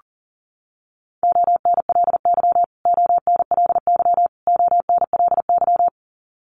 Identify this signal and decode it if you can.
Morse code, heard as a keyed tone, about 40 wpm, 700 Hz. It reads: ONLY ONLY ONLY
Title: Top 500 Words Repeated 40wpm